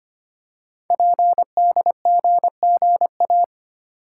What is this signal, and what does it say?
Morse code, heard as a keyed tone, about 25 wpm, 700 Hz. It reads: PDGGA
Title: Code Groups - 5 Letters - 25wpm